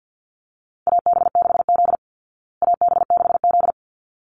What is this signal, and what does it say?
Morse code, heard as a keyed tone, about 49 wpm, 700 Hz. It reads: AB6Z AB6Z
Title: Call Signs Repeated 3x - 50wpm